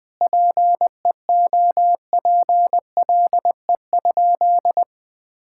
Morse code, heard as a keyed tone, about 20 wpm, 700 Hz. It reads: PEOPLE?
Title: Sentences from Top 1000 Words - Review Part 5 - 20wpm